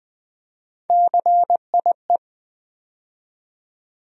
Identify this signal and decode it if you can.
Morse code, heard as a keyed tone, about 20 wpm, 700 Hz. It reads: CIE
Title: USA Call Sign Postfixes - 3 Letters 20wpm